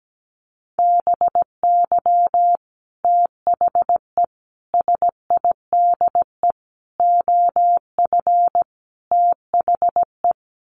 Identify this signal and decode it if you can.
Morse code, heard as a keyed tone, about 17 wpm, 700 Hz. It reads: BY THE SIDE OF THE
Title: Sets of 5 Words Encore2 17wpm